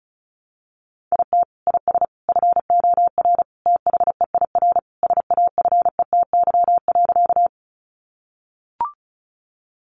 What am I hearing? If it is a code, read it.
Morse code, heard as a keyed tone, about 34 wpm, 700 Hz. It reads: IT IS FOR THEIR SAFETY.
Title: Sentences from Top 700 Words 35wpm